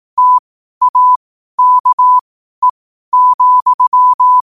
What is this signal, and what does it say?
Morse code, heard as a keyed tone, about 18 wpm, spaced as for 12 wpm, 1000 Hz. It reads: TAKE,
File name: qotdfiles / 12wpm